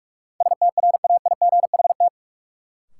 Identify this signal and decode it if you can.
Morse code, heard as a keyed tone, about 46 wpm, 700 Hz. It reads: STRAIGHT
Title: Sentences from Top 1000 Words - New Word Drill - 45wpm